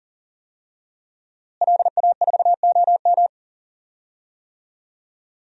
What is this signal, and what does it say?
Morse code, heard as a keyed tone, about 41 wpm, 700 Hz. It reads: LA4OM
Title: DXCC-100 Call Signs - 5 Characters - Repeated 2x - 40wpm